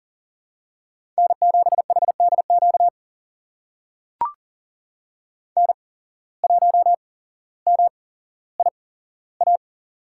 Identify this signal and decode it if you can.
Morse code, heard as a keyed tone, about 39 wpm, spaced as for 17 wpm, 700 Hz. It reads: N7HDQ N1MIA
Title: USA Call Signs - 5 Characters - 8x Character Spacing - 40wpm